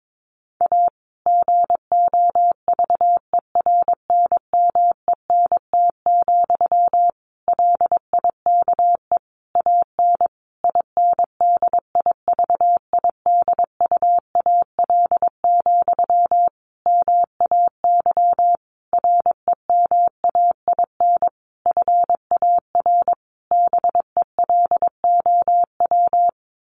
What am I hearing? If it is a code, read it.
Morse code, heard as a keyed tone, about 22 wpm, 700 Hz. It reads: A GOVERNMENT, LIKE AN INDIVIDUAL, MAY REMAIN FAR BELOW
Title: Sentences from Top 1400 Words 22wpm